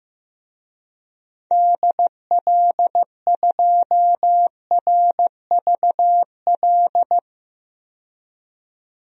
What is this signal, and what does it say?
Morse code, heard as a keyed tone, about 15 wpm, 700 Hz. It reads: DL2RVL
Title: DXCC-100 Call Signs - 6 Characters - 15wpm